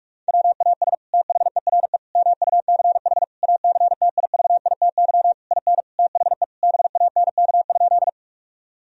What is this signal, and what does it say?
Morse code, heard as a keyed tone, about 47 wpm, 700 Hz. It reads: WAS THERE MUCH ACTIVITY IN THE BANK?
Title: Sentences from Top 1000 Words - Review Part 4 - 45wpm